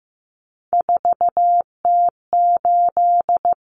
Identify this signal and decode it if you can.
Morse code, heard as a keyed tone, about 15 wpm, 700 Hz. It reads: 4T8
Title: Common Cut Numbers - 3-Digit 15wpm